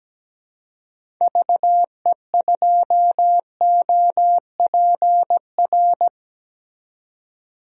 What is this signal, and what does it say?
Morse code, heard as a keyed tone, about 17 wpm, 700 Hz. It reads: VE2OPR
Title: DXCC-100 Call Signs - 6 Characters - 17wpm